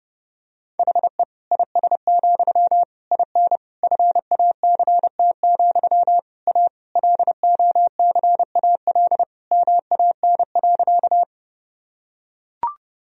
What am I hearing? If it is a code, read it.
Morse code, heard as a keyed tone, about 30 wpm, 700 Hz. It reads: HE IS, IN FACT, A LOCAL MAN.
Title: Sentences from Top 1000 Words - Review Part 3 - 30wpm